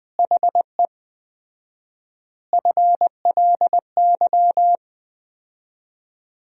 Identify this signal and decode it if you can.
Morse code, heard as a keyed tone, about 20 wpm, 700 Hz. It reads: HE FLY
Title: 4x Word-Spacing - Sentences from Top 500 Words - 20wpm